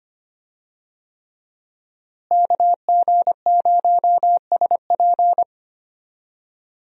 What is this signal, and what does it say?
Morse code, heard as a keyed tone, about 25 wpm, 700 Hz. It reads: KG0SP
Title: USA Call Signs - 5 Characters - Repeated 3x - 25wpm